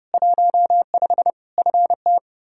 Morse code, heard as a keyed tone, about 32 wpm, 700 Hz. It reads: 15 FT
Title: Practice QSO - Exchange 2 - Info - 30wpm